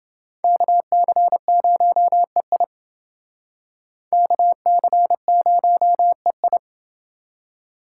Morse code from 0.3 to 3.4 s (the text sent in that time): KC0EI